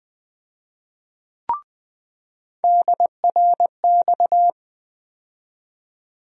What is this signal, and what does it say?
Morse code, heard as a keyed tone, about 20 wpm, 700 Hz. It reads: DRX
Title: USA Call Sign Postfixes - 3 Letters 20wpm